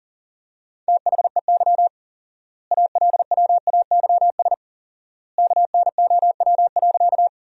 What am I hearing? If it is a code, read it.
Morse code, heard as a keyed tone, about 40 wpm, 700 Hz. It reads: THEY ALWAYS KNOW.
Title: 4x Word-Spacing - Sentences from Top 300 Words - 40wpm